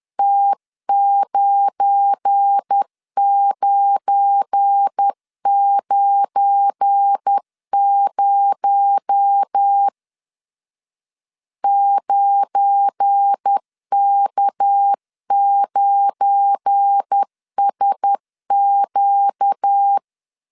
Morse code, heard as a keyed tone, about 10 wpm, 800 Hz. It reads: T9990 9K9SQ
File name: LEZIONI / 7 GF9Z / 7 GF9 10wpm 2747